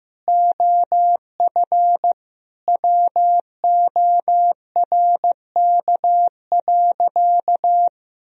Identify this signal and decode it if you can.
Morse code, heard as a keyed tone, about 15 wpm, 700 Hz. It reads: OF WORK.
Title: Sentences from Top 1000 Words - Review Part 2 - 15wpm